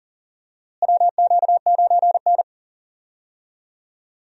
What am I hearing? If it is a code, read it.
Morse code, heard as a keyed tone, about 37 wpm, 700 Hz. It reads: WQ9N
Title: USA Call Signs - 4 Characters - Repeated 2x - 40wpm